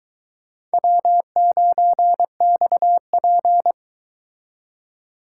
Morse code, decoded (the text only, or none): W9XP